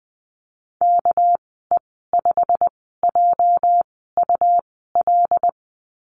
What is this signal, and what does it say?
Morse code, heard as a keyed tone, about 20 wpm, spaced as for 15 wpm, 700 Hz. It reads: KE5JUL
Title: 245 Lesson 39g - Letter z - Call Signs 20wpm